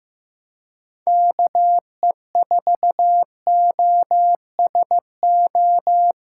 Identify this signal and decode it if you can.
Morse code, heard as a keyed tone, about 15 wpm, 700 Hz. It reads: KE4OSO
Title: USA Callsign DE Callsign - 6 Characters - 15wpm